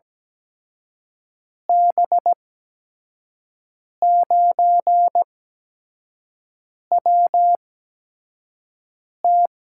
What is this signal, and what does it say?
Morse code, heard as a keyed tone, about 17 wpm, spaced as for 5 wpm, 700 Hz. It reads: B9WT
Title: USA Call Signs - 8x Character Spacing - 17wpm